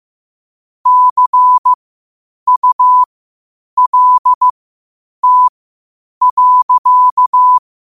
Morse code, heard as a keyed tone, about 15 wpm, spaced as for 8 wpm, 1000 Hz. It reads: CULT.
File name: qotdfiles / 8wpm